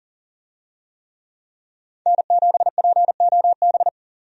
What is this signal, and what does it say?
Morse code, heard as a keyed tone, about 40 wpm, 700 Hz. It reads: N7POB
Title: USA Call Signs - 5 Characters - 40wpm